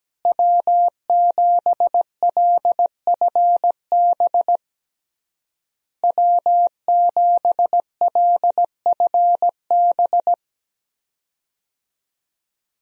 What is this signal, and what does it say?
Morse code, heard as a keyed tone, about 17 wpm, 700 Hz. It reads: W7LFB W7LFB
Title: USA Call Signs - 5 Characters - Repeated 3x - 17wpm